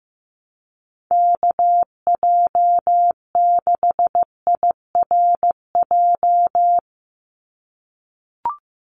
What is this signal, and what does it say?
Morse code, heard as a keyed tone, about 15 wpm, 700 Hz. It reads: KJ6IRJ
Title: USA Call Signs 15wpm